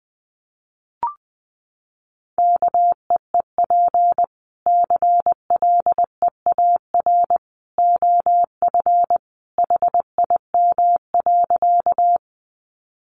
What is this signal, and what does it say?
Morse code, heard as a keyed tone, about 19 wpm, 700 Hz. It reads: KEEP CLEAR OF HIM.
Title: Sentences from Top 500 Words 20wpm